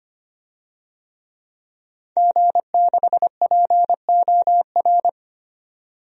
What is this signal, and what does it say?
Morse code, heard as a keyed tone, about 25 wpm, 700 Hz. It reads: G6POR
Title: DXCC-100 Call Signs - 5 Characters - 25wpm